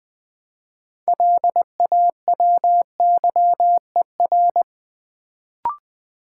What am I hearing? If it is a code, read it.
Morse code, heard as a keyed tone, about 20 wpm, 700 Hz. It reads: LAWYER
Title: Build A Word - 3 Letters At A Time - 6-Letter Words - 20wpm